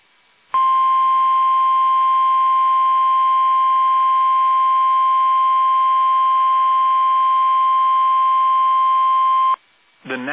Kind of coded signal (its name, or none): Emergency Alert System (SAME) tones